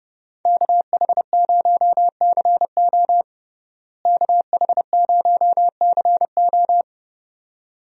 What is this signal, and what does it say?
Morse code, heard as a keyed tone, about 32 wpm, 700 Hz. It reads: KH0CO KH0CO
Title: USA Call Signs - 5 Characters - Repeated 2x - 30wpm